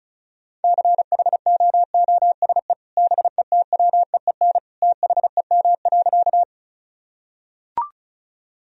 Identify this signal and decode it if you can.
Morse code, heard as a keyed tone, about 34 wpm, 700 Hz. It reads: CHOOSE BETWEEN THEM.
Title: Sentences from Top 1000 Words - Review Part 4 - 35wpm